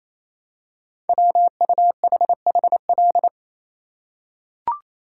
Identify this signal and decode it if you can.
Morse code, heard as a keyed tone, about 27 wpm, 700 Hz. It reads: WUHHL
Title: Code Groups - 5 Letters - 28wpm